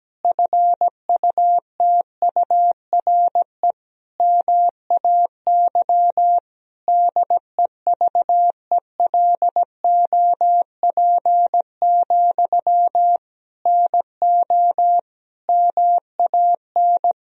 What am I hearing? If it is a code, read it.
Morse code, heard as a keyed tone, about 17 wpm, 700 Hz. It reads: FUTURE MAY DEVELOP, NO MAN
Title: Sentences from Top 1000 Words - Review Part 2 - 17wpm